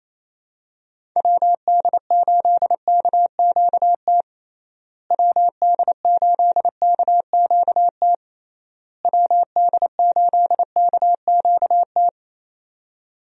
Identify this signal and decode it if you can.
Morse code, heard as a keyed tone, about 28 wpm, 700 Hz. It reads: WD8KQT WD8KQT WD8KQT
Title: USA Call Signs - 6 Characters - Repeated 3x - 28wpm